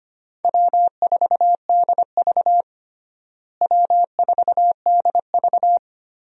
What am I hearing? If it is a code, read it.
Morse code, heard as a keyed tone, about 25 wpm, 700 Hz. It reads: W4DV W4DV
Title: USA Call Signs - 4 Characters - Repeated 2x - 25wpm